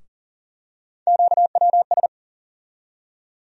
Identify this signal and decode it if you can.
Morse code, heard as a keyed tone, about 40 wpm, 700 Hz. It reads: QWS
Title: USA Call Sign Postfixes - 3 Letters 40wpm